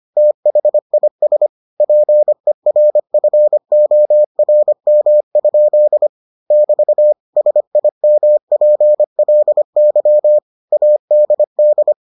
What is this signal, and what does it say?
Morse code, heard as a keyed tone, about 25 wpm, 600 Hz. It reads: THIS PERFORM? <BT> SIMPLY ADD